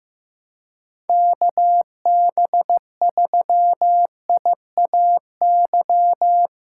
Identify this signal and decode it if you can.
Morse code, heard as a keyed tone, about 15 wpm, 700 Hz. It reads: KB3IAY
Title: USA Call Signs - 6 Characters - 2x Character Spacing - 15wpm